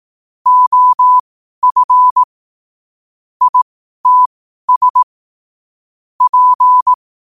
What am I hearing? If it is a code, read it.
Morse code, heard as a keyed tone, about 18 wpm, spaced as for 12 wpm, 1000 Hz. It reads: OF ITS P
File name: qotdfiles / 12wpm